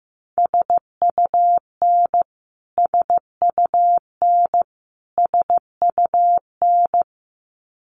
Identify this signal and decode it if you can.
Morse code, heard as a keyed tone, about 15 wpm, 700 Hz. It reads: SUN SUN SUN
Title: Top 500 Words Repeated 15wpm